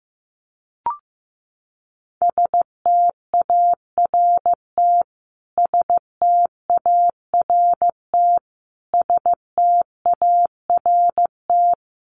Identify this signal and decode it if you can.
Morse code, heard as a keyed tone, about 15 wpm, 700 Hz. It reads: START START START
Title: Top 300 Words Repeated 15wpm